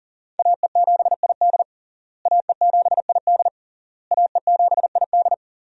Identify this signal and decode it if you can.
Morse code, heard as a keyed tone, about 40 wpm, 690 Hz. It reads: AE7ID AE7ID AE7ID
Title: USA Call Signs - 5 Characters - Repeated 3x - 40wpm